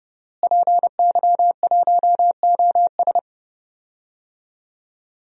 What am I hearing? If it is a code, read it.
Morse code, heard as a keyed tone, about 30 wpm, 700 Hz. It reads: PY1OS
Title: DXCC-100 Call Signs - 5 Characters - Repeated 3x - 30wpm